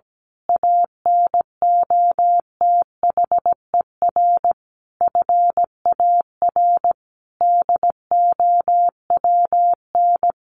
Morse code, heard as a keyed tone, about 17 wpm, 700 Hz. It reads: ANOTHER FAR DOWN
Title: A Sound Of Thunder 17wpm